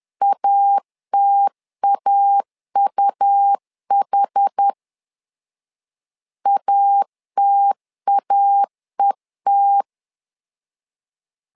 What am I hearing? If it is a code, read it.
Morse code, heard as a keyed tone, about 10 wpm, 800 Hz. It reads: ATAUH ATAET
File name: LEZIONI / 2 UHDN / 2 UH 10wpm 1714